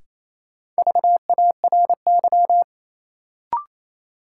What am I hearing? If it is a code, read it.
Morse code, heard as a keyed tone, about 28 wpm, 700 Hz. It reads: VARY
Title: Sentences from Top 1200 Words - New Word Drill Alt - 28wpm